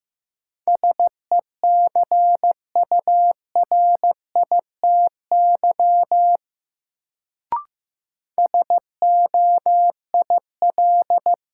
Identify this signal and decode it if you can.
Morse code, heard as a keyed tone, about 15 wpm, 700 Hz. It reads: SECURITY SOIL
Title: Sentences from Top 1000 Words - Review Part 4 - New Word Drill - 15wpm